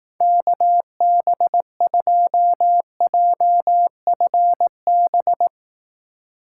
Morse code, heard as a keyed tone, about 18 wpm, 700 Hz. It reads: KB2JFB
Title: Speed Racing - USA 2x3 Call Signs 15wpm